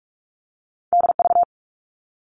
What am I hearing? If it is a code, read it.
Morse code, heard as a keyed tone, about 46 wpm, 700 Hz. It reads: DV
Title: All Cut Numbers - 2-Digit 45wpm